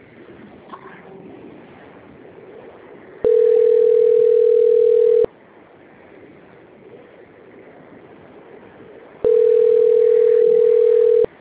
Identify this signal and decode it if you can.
Ringback tone (North America), 440 + 480 Hz.